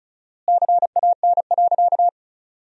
Morse code, heard as a keyed tone, about 36 wpm, 700 Hz. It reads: CAN.
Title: 4x Word-Spacing - Sentences from Top 500 Words - 35wpm